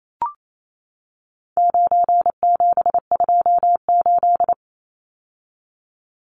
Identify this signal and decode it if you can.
Morse code, heard as a keyed tone, about 28 wpm, 700 Hz. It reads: 9728
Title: Numbers - 4-Digit 28wpm